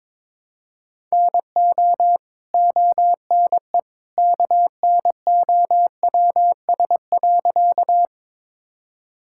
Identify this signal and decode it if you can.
Morse code, heard as a keyed tone, about 22 wpm, 700 Hz. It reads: NO ONE KNOWS.
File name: practice / 22wpm / Sentences with 2-Word Phrases - 22wpm